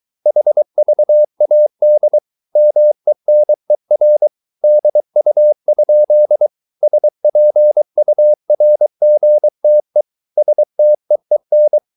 Morse code, heard as a keyed tone, about 23 wpm, 600 Hz. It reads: HVAD MENER DU? SPURGTE STEEN
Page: Ferie 23 wpm